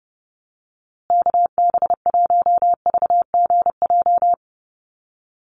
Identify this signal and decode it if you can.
Morse code, heard as a keyed tone, about 29 wpm, 700 Hz. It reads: KB1VGJ
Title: USA Call Signs Repeated 2x 30wpm